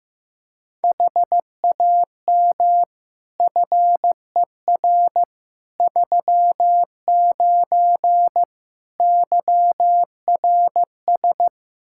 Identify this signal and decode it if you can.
Morse code, heard as a keyed tone, about 16 wpm, 700 Hz. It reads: HAM FER 39 YRS
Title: Practice QSO - Exchange 3 - More Info - 15wpm